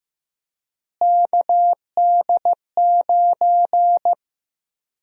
Morse code, heard as a keyed tone, about 15 wpm, 700 Hz. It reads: KD9
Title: USA Call Sign Prefixes - Repeated 2x - 15wpm